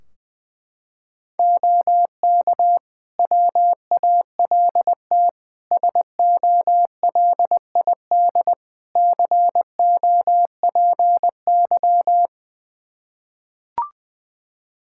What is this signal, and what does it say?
Morse code, heard as a keyed tone, about 20 wpm, 700 Hz. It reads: OK WALT SOLID COPY
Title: Practice QSO - Exchange 3 - More Info - 20wpm